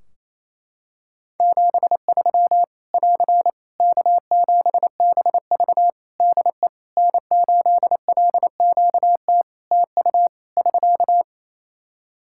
Morse code, heard as a keyed tone, about 28 wpm, 700 Hz. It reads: 73 <AR> K7BV DE N8LQT TU <SK>
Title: Practice QSO - Exchange 4 - Ending - 28wpm